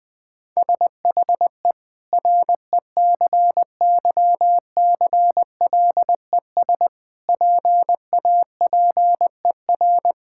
Morse code, heard as a keyed tone, about 20 wpm, 700 Hz. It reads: SHE RECYCLES PAPER
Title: Difficult Sentences with Hints - 20wpm